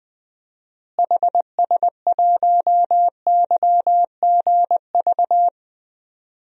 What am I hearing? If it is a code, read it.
Morse code, heard as a keyed tone, about 21 wpm, 700 Hz. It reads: HS1YGV
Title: DXCC-100 Call Signs - 6 Characters - Repeated 2x - 20wpm